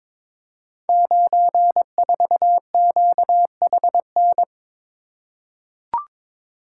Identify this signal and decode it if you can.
Morse code, heard as a keyed tone, about 22 wpm, 700 Hz. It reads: 94QHN
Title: Code Groups - 5 Letter:Numbers - 22wpm